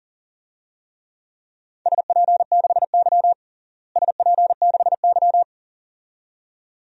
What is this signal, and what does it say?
Morse code, heard as a keyed tone, about 40 wpm, 700 Hz. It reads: SP6Y SP6Y
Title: DXCC-100 Call Signs - 4 Characters - Repeated 2x - 40wpm